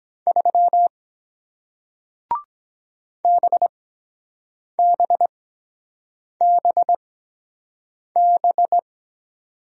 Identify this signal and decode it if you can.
Morse code, heard as a keyed tone, about 23 wpm, spaced as for 7 wpm, 700 Hz. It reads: 3 BBBB